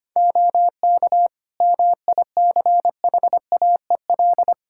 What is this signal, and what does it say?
Morse code, heard as a keyed tone, about 25 wpm, 700 Hz. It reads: OK MICHAEL
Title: Practice QSO - Exchange 3 - More Info - 25wpm